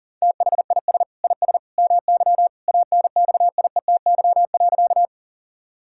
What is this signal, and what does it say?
Morse code, heard as a keyed tone, about 40 wpm, 700 Hz. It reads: THIS IS MY ANXIETY.
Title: Sentences from Top 1600 Words 40wpm